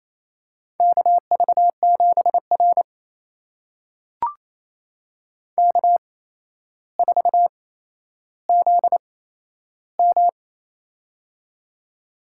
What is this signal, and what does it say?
Morse code, heard as a keyed tone, about 27 wpm, spaced as for 12 wpm, 700 Hz. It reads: KV7R K4ZM
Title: USA Call Signs - 4 Characters - 8x Character Spacing - 28wpm